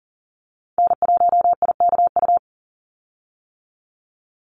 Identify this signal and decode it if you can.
Morse code, heard as a keyed tone, about 40 wpm, 700 Hz. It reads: N1IKU
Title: USA Call Signs 40wpm